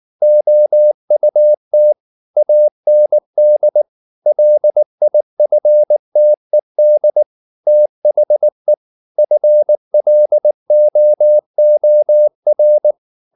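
Morse code, heard as a keyed tone, about 19 wpm, 600 Hz. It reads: OUT AND LIFTED THE FLOOR
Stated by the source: Morse code at 19 wpm from CW med Gnister, reading Ken Follett's novel Never in English.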